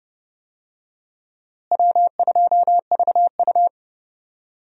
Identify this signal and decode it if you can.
Morse code, heard as a keyed tone, about 30 wpm, 700 Hz. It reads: W2VU
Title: USA Call Signs - 4 Characters - Repeated 2x - 30wpm